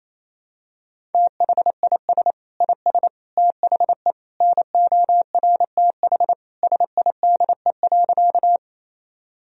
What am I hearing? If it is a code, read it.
Morse code, heard as a keyed tone, about 28 wpm, 700 Hz. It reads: THIS IS THE NORTH SIDE.
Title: Sentences from Top 1000 Words - Review Part 1 - 28wpm